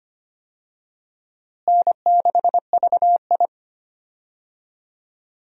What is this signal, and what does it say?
Morse code, heard as a keyed tone, about 25 wpm, 700 Hz. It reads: N6VI
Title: USA Call Signs - 4 Characters - Repeated 2x - 25wpm